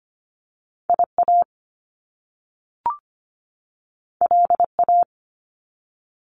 Morse code, heard as a keyed tone, about 25 wpm, 700 Hz. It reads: IA LA
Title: US States 25wpm